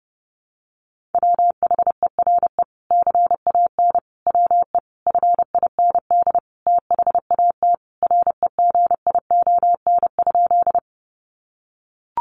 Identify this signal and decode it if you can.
Morse code, heard as a keyed tone, about 30 wpm, 700 Hz. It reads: WHERE CAN WE FIND THAT REGION?
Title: Sentences from Top 1500 Words 30wpm